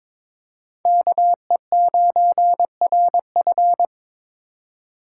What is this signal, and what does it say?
Morse code, heard as a keyed tone, about 22 wpm, 700 Hz. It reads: KE9RF
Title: USA Call Signs - 5 Characters - 2x Character Spacing - 22wpm